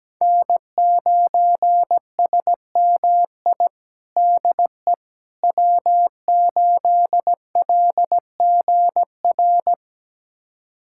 Morse code, heard as a keyed tone, about 17 wpm, 700 Hz. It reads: N9SMI DE W8LGR
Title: USA Callsign DE Callsign - 5 Characters - 17wpm